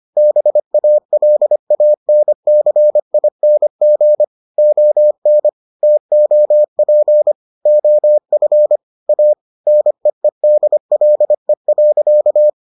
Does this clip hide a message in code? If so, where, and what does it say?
Yes: Morse code from 0.2 to 12.5 s, heard as a keyed tone, about 25 wpm, 600 Hz. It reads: BALANCING ON TOP OF A NEEDLE.